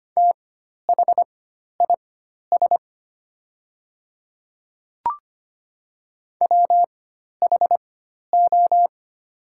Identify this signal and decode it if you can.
Morse code, heard as a keyed tone, about 25 wpm, spaced as for 12 wpm, 700 Hz. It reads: THIS WHO
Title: 4x Character Spacing - Top 300 Words - 25wpm